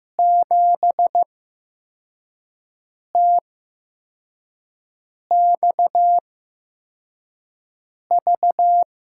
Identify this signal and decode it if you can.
Morse code, heard as a keyed tone, about 15 wpm, spaced as for 4 wpm, 700 Hz. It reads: 7TXV
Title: USA Call Signs - 6 Characters - 8x Character Spacing - 15wpm